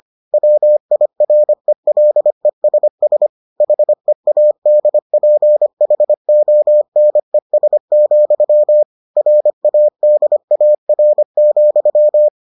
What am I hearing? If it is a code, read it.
Morse code, heard as a keyed tone, about 25 wpm, 600 Hz. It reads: WIRELESS HEADPHONES, RADAR,